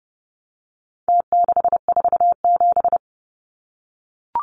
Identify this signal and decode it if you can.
Morse code, heard as a keyed tone, about 30 wpm, 700 Hz. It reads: T647
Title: Numbers - Contest Style - 4-Digit 30wpm